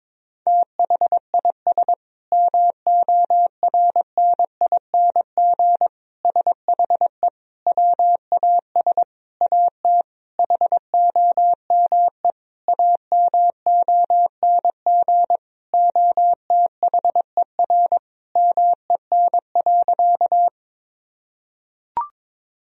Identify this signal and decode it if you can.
Morse code, heard as a keyed tone, about 22 wpm, 700 Hz. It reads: THIS MORNING SHE WAS AT HOME AMONG OTHER MEN.
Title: Sentences from Top 1000 Words - Review Part 3 - 22wpm